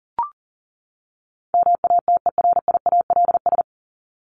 Morse code, heard as a keyed tone, about 40 wpm, 700 Hz. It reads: MATERIALS
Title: ES:S Suffix - Top 100 Words with Base Words 40wpm